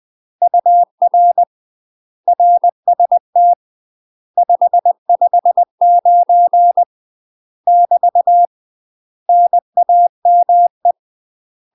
Morse code, heard as a keyed tone, about 20 wpm, 700 Hz. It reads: UR RST 559 <BT> NAME
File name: UK MASTER QSOs TRAINING / 20WPM / UK MASTER QSOs 0044